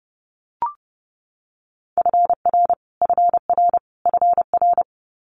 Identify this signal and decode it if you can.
Morse code, heard as a keyed tone, about 32 wpm, 700 Hz. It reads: FR FR FR
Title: Multi-letter Phonemes Repeated 3x 30wpm